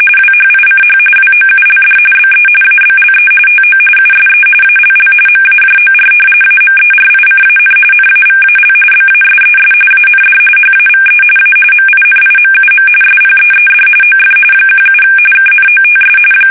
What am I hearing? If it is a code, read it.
RTTY, about 45.45 baud. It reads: WHICH WOULD BE ABLE TO USE ALL MAJOR AMERICAN NAVAL BASES AND HAVE A MINIMUM SPEED OF 17 KNOTS
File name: fsk850 / battleships